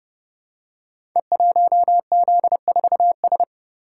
Morse code, heard as a keyed tone, about 30 wpm, 700 Hz. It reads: E1Z4S
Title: Code Groups - 5 Letter:Numbers - 30wpm